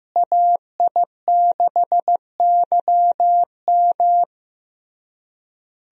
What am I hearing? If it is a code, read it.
Morse code, heard as a keyed tone, about 15 wpm, 700 Hz. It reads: AI6YM
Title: USA Call Signs - 5 Characters - Repeated 3x - 15wpm